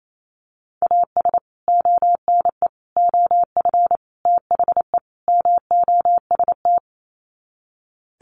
Morse code, heard as a keyed tone, about 28 wpm, 700 Hz. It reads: AS ONE OF THE MOST
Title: Sets of 5 Words 28wpm